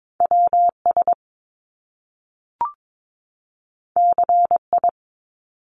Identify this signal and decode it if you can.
Morse code, heard as a keyed tone, about 22 wpm, 700 Hz. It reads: WS CI
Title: Top Two Letter Combinations in Words 22wpm